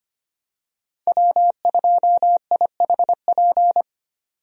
Morse code, heard as a keyed tone, about 25 wpm, 700 Hz. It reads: W2IHP
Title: USA Call Signs - 5 Characters - 25wpm